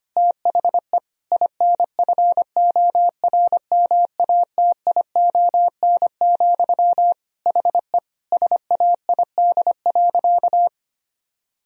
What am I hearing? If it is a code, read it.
Morse code, heard as a keyed tone, about 25 wpm, 700 Hz. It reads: THE INFORMATION, HE SAID.
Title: Sentences from Top 1000 Words - Review Part 3 - 25wpm